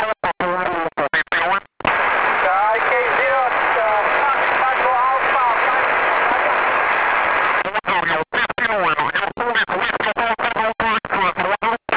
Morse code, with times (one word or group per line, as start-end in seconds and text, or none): none